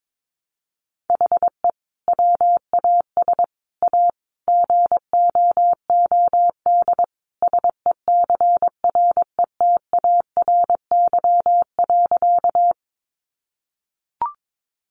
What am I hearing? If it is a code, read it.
Morse code, heard as a keyed tone, about 22 wpm, 700 Hz. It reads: HE WAS A GOOD SECRETARY.
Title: Sentences from Top 1300 Words 22wpm